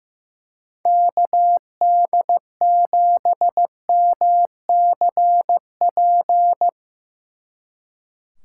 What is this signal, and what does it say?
Morse code, heard as a keyed tone, about 17 wpm, 700 Hz. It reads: KD7MCP
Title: Speed Racing - USA 2x3 Call Signs 15wpm